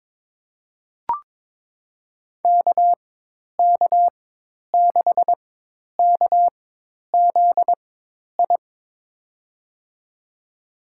Morse code, heard as a keyed tone, about 22 wpm, spaced as for 10 wpm, 700 Hz. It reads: KK6KZI
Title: USA Call Signs - 6 Characters - 4x Character Spacing - 22wpm